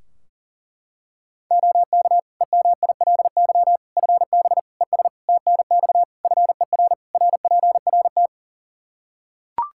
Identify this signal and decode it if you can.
Morse code, heard as a keyed tone, about 40 wpm, 700 Hz. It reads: OK EMILY FB ES TNX FER RPRT
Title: Practice QSO - Exchange 2 - Info - 40wpm